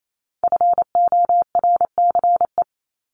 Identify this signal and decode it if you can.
Morse code, heard as a keyed tone, about 28 wpm, 700 Hz. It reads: FORCE
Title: ES:S Suffix - Top 100 Words with Base Words 28wpm